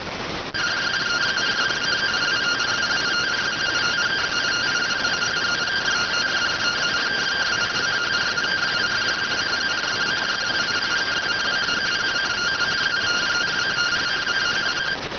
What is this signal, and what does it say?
Two-tone teleprinter signal, about 170 Hz shift.